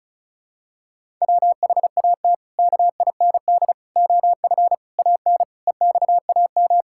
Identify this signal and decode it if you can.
Morse code, heard as a keyed tone, about 36 wpm, 700 Hz. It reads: WHAT KIND OF AN EXAM
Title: Sentences from Top 1000 Words - Review Part 5 - 35wpm